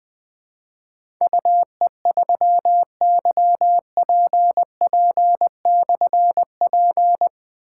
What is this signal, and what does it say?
Morse code, heard as a keyed tone, about 20 wpm, 700 Hz. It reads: UE3YPP/P
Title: Difficult Call Signs 20wpm